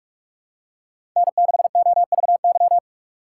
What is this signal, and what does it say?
Morse code, heard as a keyed tone, about 42 wpm, 700 Hz. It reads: N6OVY
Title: USA Call Signs - 5 Characters - 4x Character Spacing - 45wpm